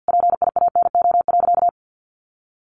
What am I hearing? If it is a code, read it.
Morse code, heard as a keyed tone, about 50 wpm, 700 Hz. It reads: PIANO.
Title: Additional Easy Phrases 50wpm